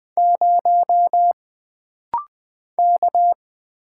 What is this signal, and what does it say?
Morse code, heard as a keyed tone, about 20 wpm, 700 Hz. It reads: 0 K
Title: Single Letter-Number - Rapid-Fire 20wpm